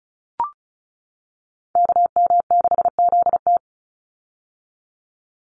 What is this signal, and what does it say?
Morse code, heard as a keyed tone, about 34 wpm, 700 Hz. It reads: KM6ZT
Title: Call Signs - 35wpm